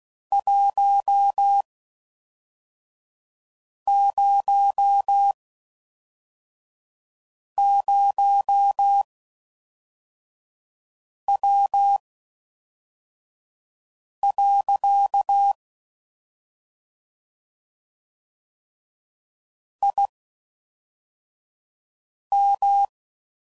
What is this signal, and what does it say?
Morse code, heard as a keyed tone, about 16 wpm, spaced as for 4 wpm, 800 Hz. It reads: 100W. IM